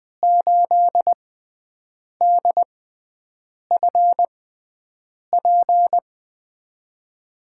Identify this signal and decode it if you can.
Morse code, heard as a keyed tone, about 20 wpm, spaced as for 7 wpm, 700 Hz. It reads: 8DFP
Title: USA Call Signs - 6 Characters - 6x Character Spacing - 20wpm